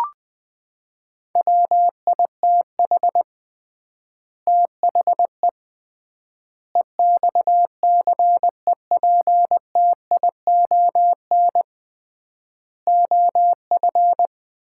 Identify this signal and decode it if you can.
Morse code, heard as a keyed tone, about 20 wpm, 700 Hz. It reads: WITH THE EXCEPTION OF
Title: 3x Word-Spacing - Sets of 4 Words - 20wpm